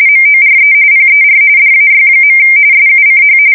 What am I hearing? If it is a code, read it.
RTTY, about 45.45 baud. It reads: MORE PER DAY. SUCH
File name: fsk170 / printing-telegraph